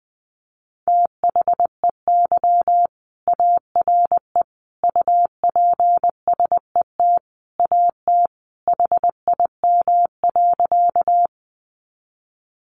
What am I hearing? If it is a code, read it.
Morse code, heard as a keyed tone, about 20 wpm, 700 Hz. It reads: THEY ARE UPSET AT HIM.
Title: Sentences from Top 800 Words 20wpm